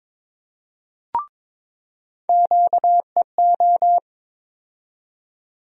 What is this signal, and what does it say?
Morse code, heard as a keyed tone, about 22 wpm, 700 Hz. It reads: QEO
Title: USA Call Sign Postfixes - 3 Letters 22wpm